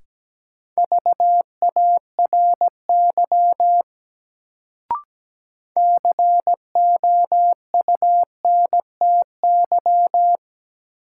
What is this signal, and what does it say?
Morse code, heard as a keyed tone, about 17 wpm, 700 Hz. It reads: VARY COUNTY
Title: Sentences from Top 1200 Words - New Word Drill Alt - 17wpm